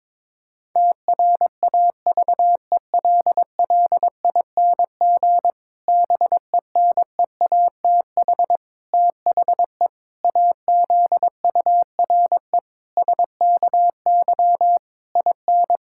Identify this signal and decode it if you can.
Morse code, heard as a keyed tone, about 22 wpm, 700 Hz. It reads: TRAVELLING BENEATH THE AZURE SKY IN
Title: Pangram Challenge v2 - 22wpm